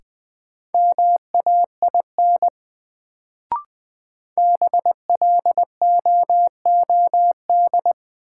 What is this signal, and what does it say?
Morse code, heard as a keyed tone, about 20 wpm, 700 Hz. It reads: MAIN BLOOD
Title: Sentences from Top 1000 Words - Review Part 2 - New Word Drill - 20wpm